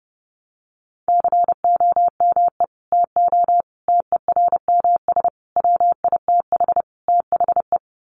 Morse code, heard as a keyed tone, about 30 wpm, 700 Hz. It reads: COME TO TERMS WITH THE
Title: Sets of 5 Words Encore2 30wpm